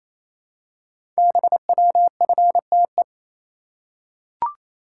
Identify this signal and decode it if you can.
Morse code, heard as a keyed tone, about 27 wpm, 700 Hz. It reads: BWFTE
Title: Code Groups - 5 Letters - 28wpm